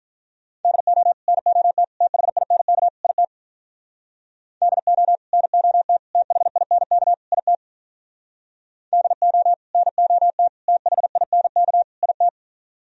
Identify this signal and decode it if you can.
Morse code, heard as a keyed tone, about 44 wpm, 700 Hz. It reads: DO NOT THINK IT DO NOT THINK IT DO NOT THINK IT